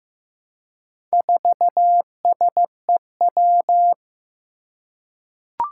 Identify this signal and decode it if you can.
Morse code, heard as a keyed tone, about 15 wpm, 700 Hz. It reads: 4SEW
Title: Four Challenging Letter-Number Combinations 15wpm